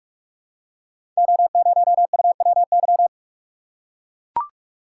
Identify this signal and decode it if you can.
Morse code, heard as a keyed tone, about 44 wpm, 700 Hz. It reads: O0UWY
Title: Code Groups - 5 Letter:Numbers - 45wpm